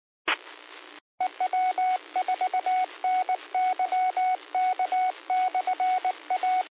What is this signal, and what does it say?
Morse code, heard as a keyed tone, about 19 wpm, 740 Hz, with radio noise. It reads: EW4NYK/A
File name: qsorec SimplexLogic